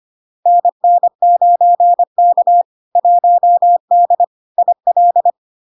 Morse code, heard as a keyed tone, about 25 wpm, 700 Hz. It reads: NN9K 1D IL
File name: LARK-FD-2022-Log-in-CW-25wpm-700-Hz